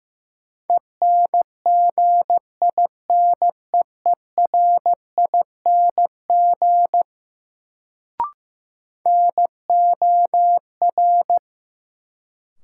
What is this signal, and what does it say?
Morse code, heard as a keyed tone, about 15 wpm, 700 Hz. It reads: ENGINEERING NOR
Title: Sentences from Top 1500 Words - New Word Drill - 15wpm